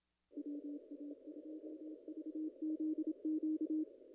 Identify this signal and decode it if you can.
Morse code, heard as a keyed tone, about 26 wpm, 310 Hz. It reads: WA2VZQ
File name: contests / ss-cw-k5tr-op-n5rz -0000